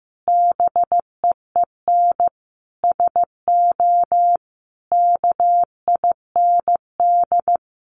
Morse code, heard as a keyed tone, about 15 wpm, 700 Hz. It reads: BEEN SO KIND
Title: Sentences from Top 1200 Words 15wpm